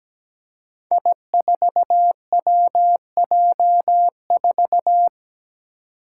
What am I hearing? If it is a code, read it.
Morse code, heard as a keyed tone, about 17 wpm, 700 Hz. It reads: I4WJ4
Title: Five Challenging Letter-Number Combinations 17wpm